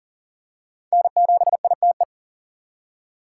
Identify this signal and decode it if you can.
Morse code, heard as a keyed tone, about 40 wpm, 700 Hz. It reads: N7ITE
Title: USA Call Signs - 5 Characters - 3x Character Spacing - 40wpm